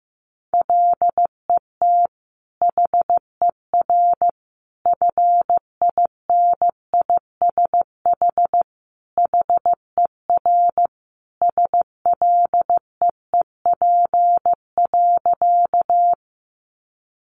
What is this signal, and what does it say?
Morse code, heard as a keyed tone, about 14 wpm, 700 Hz. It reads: LET HER FINISH HER SLEEP.
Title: Sentences from Top 1200 Words 15wpm